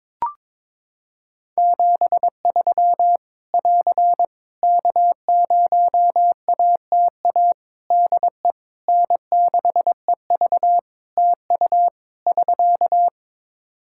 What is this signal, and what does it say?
Morse code, heard as a keyed tone, about 22 wpm, 700 Hz. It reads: 73 <AR> K0ATA DE N6EV TU <SK>
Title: Practice QSO - Exchange 4 - Ending - 22wpm